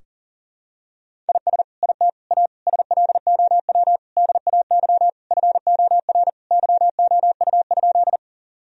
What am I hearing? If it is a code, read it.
Morse code, heard as a keyed tone, about 40 wpm, 700 Hz. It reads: IS IT A SLOW DAY FOR YOU?
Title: Sentences from Top 1000 Words - Review Part 2 - 40wpm